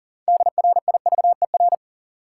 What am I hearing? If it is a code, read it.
Morse code, heard as a keyed tone, about 40 wpm, 700 Hz. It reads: DRIVER
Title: Sentences from Top 1000 Words - New Word Drill - 40wpm